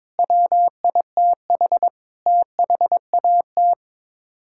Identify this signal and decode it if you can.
Morse code, heard as a keyed tone, about 22 wpm, 700 Hz. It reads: WITH THAT
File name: practice / 22wpm / Sets of 7 Words - 22wpm